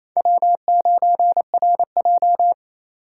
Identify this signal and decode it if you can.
Morse code, heard as a keyed tone, about 28 wpm, 700 Hz. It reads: W9RJ
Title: USA Call Signs - 3x Character Spacing - 28wpm